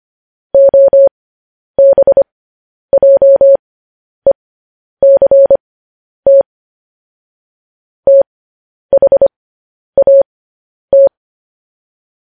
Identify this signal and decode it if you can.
Morse code, heard as a keyed tone, about 25 wpm, spaced as for 10 wpm, 550 Hz. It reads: OBJECT THAT
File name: podcast / News headlines-1x-25x10